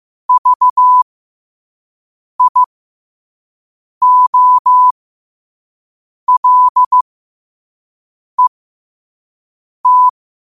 Morse code, heard as a keyed tone, about 15 wpm, spaced as for 5 wpm, 1000 Hz. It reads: VIOLET
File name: qotdfiles / 5wpm